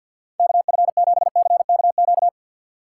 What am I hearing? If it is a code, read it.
Morse code, heard as a keyed tone, about 59 wpm, 700 Hz. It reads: KF7CKQ
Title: Speed Racing - USA 2x3 Call Signs 50wpm